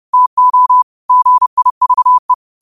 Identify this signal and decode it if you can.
Morse code, heard as a keyed tone, about 30 wpm, 1000 Hz. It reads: TO GIVE
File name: qotdfiles / 30wpm